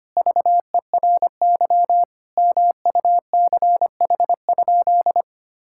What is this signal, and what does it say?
Morse code, heard as a keyed tone, about 25 wpm, 700 Hz. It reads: VERY MUCH?
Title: Sentences from Top 1000 Words - Review Part 5 - 25wpm